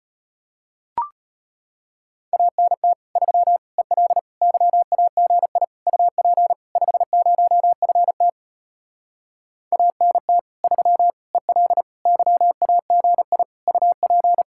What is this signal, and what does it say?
Morse code, heard as a keyed tone, about 33 wpm, 700 Hz. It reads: ANT 3 EL YAGI UP 50FT ANT 3 EL YAGI UP